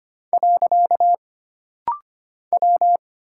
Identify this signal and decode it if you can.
Morse code, heard as a keyed tone, about 25 wpm, 700 Hz. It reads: . W
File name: Single Character (Letter:Number:Punctuation:Prosign) - Rapid-Fire - 25wpm